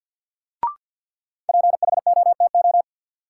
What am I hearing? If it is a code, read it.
Morse code, heard as a keyed tone, about 49 wpm, 700 Hz. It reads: PHOTO
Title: Sentences from Top 1200 Words - New Word Drill Alt - 50wpm